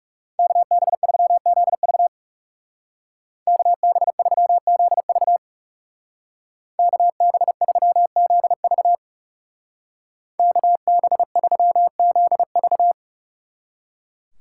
Morse code, heard as a keyed tone, about 36 wpm, 700 Hz. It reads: KB3ZV KB3ZV KB3ZV KB3ZV
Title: Speed Racing - USA 2x2 Call Signs 30wpm